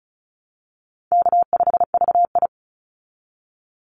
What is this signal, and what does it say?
Morse code, heard as a keyed tone, about 35 wpm, 700 Hz. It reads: K5VI
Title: Easy Call Signs Repeated 3x - 35wpm